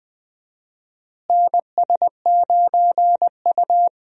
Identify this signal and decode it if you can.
Morse code, heard as a keyed tone, about 20 wpm, 700 Hz. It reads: NS9U
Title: USA Call Signs - 4 Characters - Repeated 2x - 20wpm